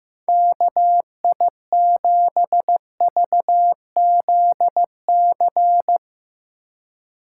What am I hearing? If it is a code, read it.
Morse code, heard as a keyed tone, about 15 wpm, 700 Hz. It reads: KI7VZC
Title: USA Call Signs - 6x Character Spacing - 15wpm